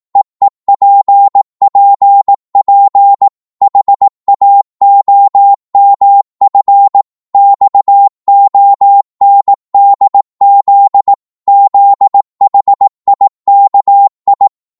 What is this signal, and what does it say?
Morse code, heard as a keyed tone, about 18 wpm, 800 Hz. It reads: EEPPP HAOMF XONDZ ZHIKI
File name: Morse / ltr-18wpm